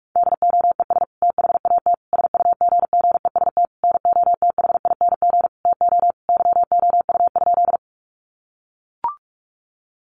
Morse code, heard as a keyed tone, about 44 wpm, 700 Hz. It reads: DOES THAT SUGGEST NOTHING TO YOU?
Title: Sentences from Top 1300 Words 45wpm